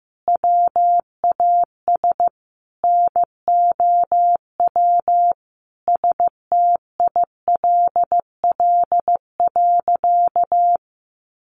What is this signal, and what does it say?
Morse code, heard as a keyed tone, about 15 wpm, 700 Hz. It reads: WAS NOW STILL.
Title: Sentences from Top 1500 Words 15wpm